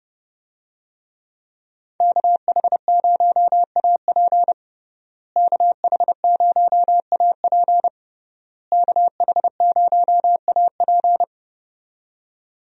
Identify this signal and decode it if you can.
Morse code, heard as a keyed tone, about 31 wpm, 700 Hz. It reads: KH0AP KH0AP KH0AP
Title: USA Call Signs - 5 Characters - Repeated 3x - 30wpm